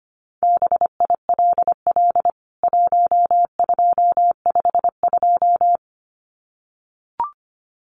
Morse code, heard as a keyed tone, about 25 wpm, 700 Hz. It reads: BILL 1252
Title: CWT Exchange 25wpm